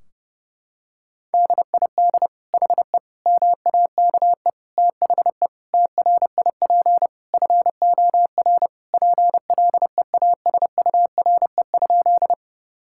Morse code, heard as a keyed tone, about 30 wpm, 700 Hz. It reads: DID HE MAKE THE TRIP FOR PLEASURE?
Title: Sentences from Top 1600 Words 30wpm